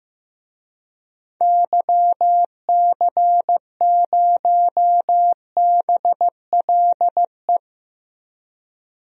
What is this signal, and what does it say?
Morse code, heard as a keyed tone, about 15 wpm, 700 Hz. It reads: YC0BLE
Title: DXCC-100 Call Signs - 6 Characters - Repeated 3x - 15wpm